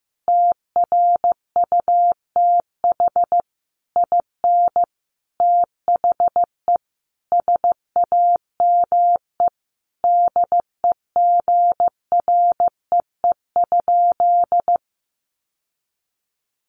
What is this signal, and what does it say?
Morse code, heard as a keyed tone, about 15 wpm, 700 Hz. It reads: TRUTH IN THE SAME DEGREE?
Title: Sentences from Top 1500 Words 15wpm